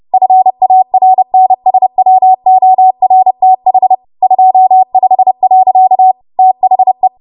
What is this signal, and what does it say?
Morse code, heard as a keyed tone, about 30 wpm, 750 Hz. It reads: FARNSWORTH 25. THE
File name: Morse / 30WPM